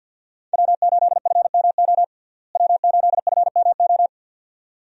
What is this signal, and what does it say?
Morse code, heard as a keyed tone, about 48 wpm, 700 Hz. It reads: W8FMO W8FMO
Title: USA Call Signs - 5 Characters - Repeated 2x - 50wpm